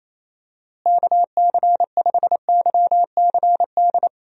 Morse code, heard as a keyed tone, about 28 wpm, 700 Hz. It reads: KC5YCD
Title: USA Call Signs - 6 Characters - Repeated 2x - 28wpm